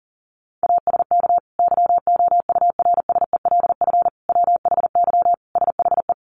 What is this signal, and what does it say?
Morse code, heard as a keyed tone, about 40 wpm, 700 Hz. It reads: ASK YOURSELF WHY SHE
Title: Sentences from Top 600 Words 40wpm